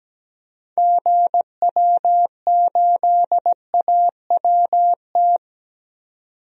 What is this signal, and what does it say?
Morse code, heard as a keyed tone, about 17 wpm, 700 Hz. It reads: GW8AWT
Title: DXCC-100 Call Signs - 6 Characters - 17wpm